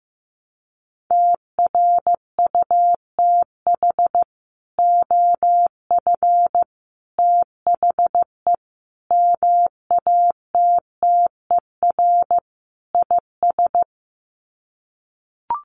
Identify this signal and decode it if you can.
Morse code, heard as a keyed tone, about 15 wpm, 700 Hz. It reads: TRUTH OF THE MATTER IS
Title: Sets of 5 Words Encore2 15wpm